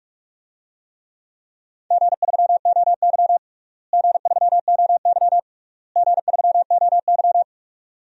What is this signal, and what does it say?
Morse code, heard as a keyed tone, about 46 wpm, 700 Hz. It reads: G3OY G3OY G3OY
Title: DXCC-100 Call Signs - 4 Characters - Repeated 3x - 45wpm